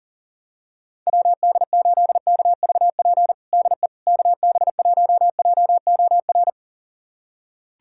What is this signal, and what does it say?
Morse code, heard as a keyed tone, about 40 wpm, 700 Hz. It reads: WD8KVP DE KB1JOR
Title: USA Callsign DE Callsign - 6 Characters - 40wpm